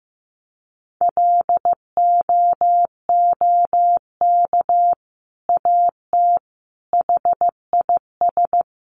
Morse code, heard as a keyed tone, about 15 wpm, 700 Hz. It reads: LOOK AT HIS
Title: Sentences from Top 1300 Words 15wpm